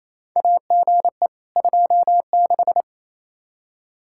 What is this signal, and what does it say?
Morse code, heard as a keyed tone, about 27 wpm, 700 Hz. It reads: AGE 26
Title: Practice QSO - Exchange 3 - More Info - 28wpm